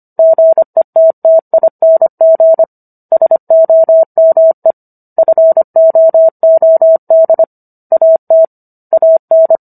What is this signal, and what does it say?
Morse code, heard as a keyed tone, about 25 wpm, 650 Hz. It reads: GETTING SOME FOOD AT AN
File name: Book One / 16-Chapter 16 - 25 WPM